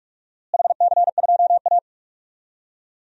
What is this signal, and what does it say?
Morse code, heard as a keyed tone, about 46 wpm, 700 Hz. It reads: HC2A
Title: DXCC-100 Call Signs - 4 Characters - Repeated 3x - 45wpm